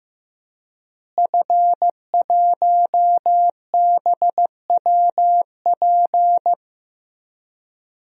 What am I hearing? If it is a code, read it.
Morse code, heard as a keyed tone, about 15 wpm, 700 Hz. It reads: F1BWP
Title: DXCC-100 Call Signs - 5 Characters - Repeated 2x - 15wpm